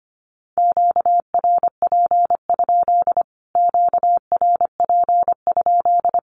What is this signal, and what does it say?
Morse code, heard as a keyed tone, about 25 wpm, 700 Hz. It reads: QRP? QRP?
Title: Q-Codes Repeated 3x 25wpm